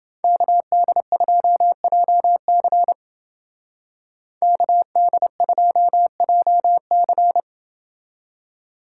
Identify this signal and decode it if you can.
Morse code, heard as a keyed tone, about 29 wpm, 700 Hz. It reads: KD2JC KD2JC